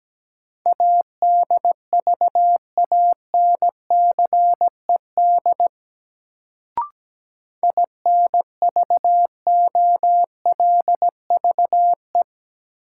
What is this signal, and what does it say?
Morse code, heard as a keyed tone, about 17 wpm, 700 Hz. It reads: ADVANCED INVOLVE
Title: Sentences from Top 1200 Words - New Word Drill - 17wpm